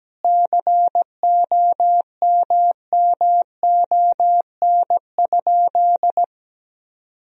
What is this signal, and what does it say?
Morse code, heard as a keyed tone, about 17 wpm, 700 Hz. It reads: COMMON?
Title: Sentences from Top 1000 Words - Review Part 3 - 17wpm